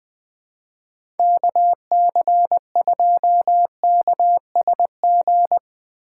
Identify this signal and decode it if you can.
Morse code, heard as a keyed tone, about 21 wpm, 700 Hz. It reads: KC2KSG
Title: USA Call Signs - 6 Characters - 6x Character Spacing - 20wpm